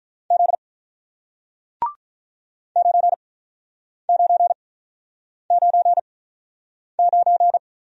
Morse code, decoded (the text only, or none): Z 9999